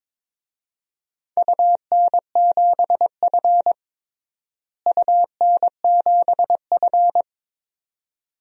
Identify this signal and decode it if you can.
Morse code, heard as a keyed tone, about 22 wpm, 700 Hz. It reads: UN7F UN7F
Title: DXCC-100 Call Signs - 4 Characters - Repeated 2x - 22wpm